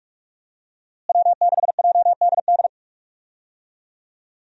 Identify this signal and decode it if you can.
Morse code, heard as a keyed tone, about 43 wpm, 700 Hz. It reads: W6JDD
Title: USA Call Signs - 5 Characters - Repeated 2x - 45wpm